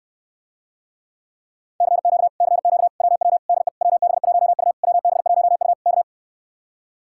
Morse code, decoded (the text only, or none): CQ CQ WW DE WB1U WB1U K